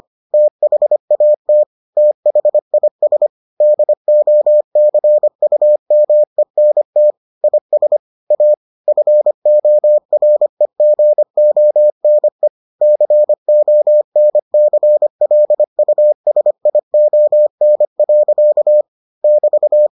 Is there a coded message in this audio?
Yes: Morse code from 0.3 to 19.9 s, heard as a keyed tone, about 25 wpm, 600 Hz. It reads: THAT THIS DOCUMENT IS A FOREGONE CONCLUSION. <BT>